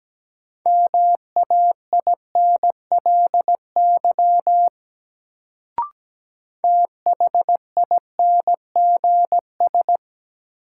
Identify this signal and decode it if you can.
Morse code, heard as a keyed tone, about 17 wpm, 700 Hz. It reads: MAINLY THINGS
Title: Sentences from Top 1400 Words - New Word Drill Alt - 17wpm